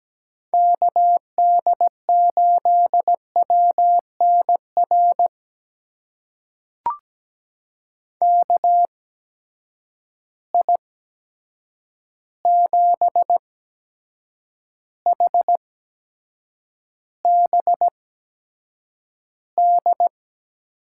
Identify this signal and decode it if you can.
Morse code, heard as a keyed tone, about 17 wpm, spaced as for 7 wpm, 700 Hz. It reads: KD8WNR KI7HBD
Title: USA Call Signs - 6 Characters - 8x Character Spacing - 17wpm